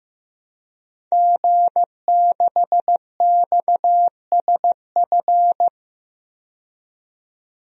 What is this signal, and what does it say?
Morse code, heard as a keyed tone, about 15 wpm, 700 Hz. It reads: G6XSF
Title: DXCC-100 Call Signs - 5 Characters - 15wpm